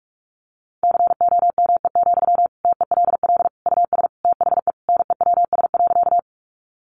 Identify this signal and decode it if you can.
Morse code, heard as a keyed tone, about 44 wpm, 700 Hz. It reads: COME, TELL US THE NEWS.
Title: Sentences from Top 800 Words 45wpm